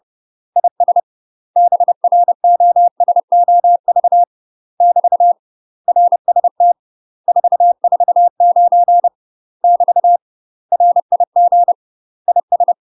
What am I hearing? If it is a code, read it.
Morse code, heard as a keyed tone, about 30 wpm, 700 Hz. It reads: IS BROSOV <BT> RST 449 <BT> RIG IS
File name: UK MASTER QSOs TRAINING/30WPM